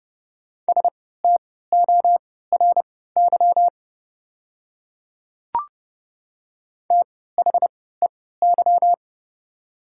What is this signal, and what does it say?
Morse code, heard as a keyed tone, about 30 wpm, spaced as for 17 wpm, 700 Hz. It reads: STORY THEY
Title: 3x Character Spacing - Top 300 Words - 30wpm